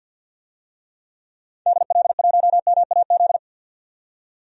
Morse code, heard as a keyed tone, about 51 wpm, 700 Hz. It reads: DL1MAZ
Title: DXCC-100 Call Signs - 6 Characters - 50wpm